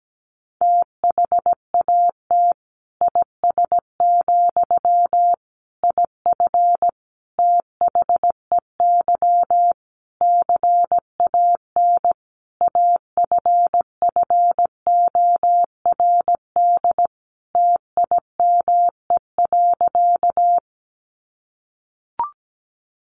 Morse code, heard as a keyed tone, about 17 wpm, 700 Hz. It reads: THAT IS, IF THEY CAN AFFORD TIME.
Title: Sentences from Top 1100 Words 17wpm